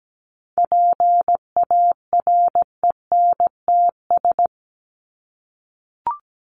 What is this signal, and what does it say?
Morse code, heard as a keyed tone, about 17 wpm, 700 Hz. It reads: PARENTS
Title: ES:S Suffix - Top 100 Words with Base Words 17wpm